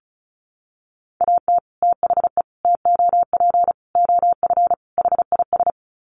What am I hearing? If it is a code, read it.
Morse code, heard as a keyed tone, about 34 wpm, 700 Hz. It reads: AT THE TOP OF HIS
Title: Sets of 5 Words Encore 35wpm